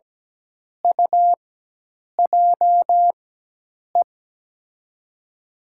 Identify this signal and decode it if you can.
Morse code, heard as a keyed tone, about 17 wpm, spaced as for 8 wpm, 700 Hz. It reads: UJE
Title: USA Call Signs - 4x Character Spacing - 17wpm